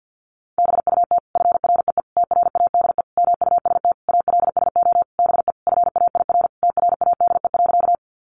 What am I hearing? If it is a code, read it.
Morse code, heard as a keyed tone, about 50 wpm, 700 Hz. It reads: BUT FREE TRADE MUST ALSO BE FAIR TRADE.
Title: Sentences from Top 1300 Words 50wpm